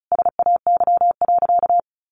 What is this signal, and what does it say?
Morse code, heard as a keyed tone, about 34 wpm, 700 Hz. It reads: SAY.
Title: Sentences from Top 600 Words 35wpm